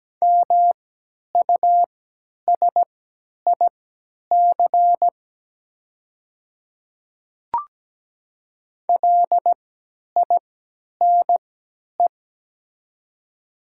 Morse code, heard as a keyed tone, about 17 wpm, spaced as for 10 wpm, 700 Hz. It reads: MUSIC LINE
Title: 3x Character Spacing - Top 300 Words - 17wpm